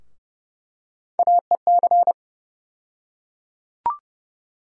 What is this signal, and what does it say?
Morse code, heard as a keyed tone, about 31 wpm, 700 Hz. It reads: AEC
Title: USA Call Sign Postfixes - 3 Letters 30wpm